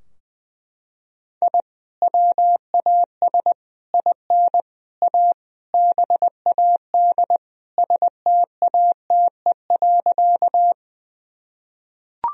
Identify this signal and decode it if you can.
Morse code, heard as a keyed tone, about 20 wpm, 700 Hz. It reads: I WAS IN A BAD STATE.
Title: Sentences from Top 1000 Words - Review Part 2 - 20wpm